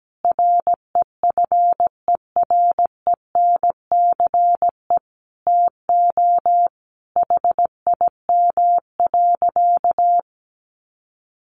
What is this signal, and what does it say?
Morse code, heard as a keyed tone, about 17 wpm, 700 Hz. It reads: REFERENCE TO HIM.
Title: Sentences from Top 1200 Words 17wpm